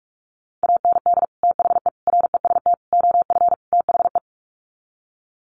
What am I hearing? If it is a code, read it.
Morse code, heard as a keyed tone, about 44 wpm, 700 Hz. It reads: AND THE REST OF THE
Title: Sets of 5 Words 45wpm